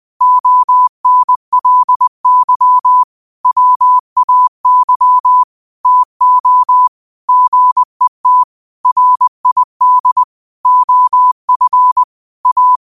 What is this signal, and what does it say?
Morse code, heard as a keyed tone, about 20 wpm, 1000 Hz. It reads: ONLY WAY TO GET RID OF A
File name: qotdfiles / 20wpm